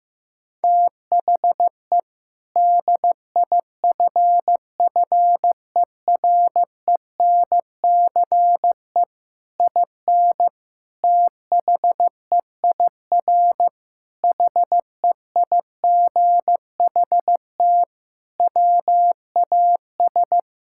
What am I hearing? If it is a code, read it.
Morse code, heard as a keyed tone, about 16 wpm, 700 Hz. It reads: THE DIFFERENCE IN THEIR HEIGHT WAS
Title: Sentences from Top 1600 Words 15wpm